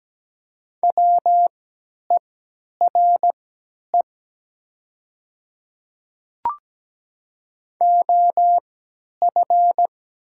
Morse code, heard as a keyed tone, about 17 wpm, spaced as for 10 wpm, 700 Hz. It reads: WERE OF
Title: 3x Character Spacing - Top 200 Words - 17wpm